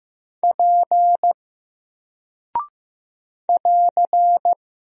Morse code, heard as a keyed tone, about 15 wpm, 700 Hz. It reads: P <AR>
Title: Single Character (Letter:Number:Punctuation:Prosign) 15wpm